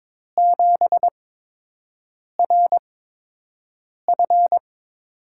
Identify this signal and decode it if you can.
Morse code, heard as a keyed tone, about 22 wpm, spaced as for 6 wpm, 700 Hz. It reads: 7RF
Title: USA Call Signs - 5 Characters - 8x Character Spacing - 22wpm